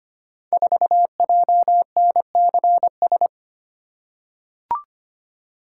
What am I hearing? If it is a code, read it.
Morse code, heard as a keyed tone, about 25 wpm, 700 Hz. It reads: 4JNCS
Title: Code Groups - 5 Letter:Numbers - 25wpm